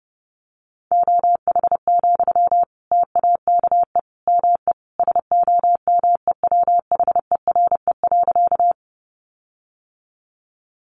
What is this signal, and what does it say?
Morse code, heard as a keyed tone, about 30 wpm, 700 Hz. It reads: OH, TAKE ME SOMEWHERE.
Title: Sentences from Top 1400 Words 30wpm